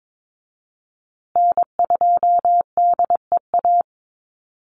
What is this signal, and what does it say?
Morse code, heard as a keyed tone, about 22 wpm, 700 Hz. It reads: N2DEA
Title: USA Call Signs Repeated 3x 22wpm